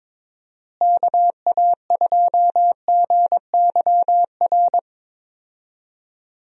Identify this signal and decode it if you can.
Morse code, heard as a keyed tone, about 22 wpm, 700 Hz. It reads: KA2GYR
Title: USA Call Signs - 6 Characters - Repeated 3x - 22wpm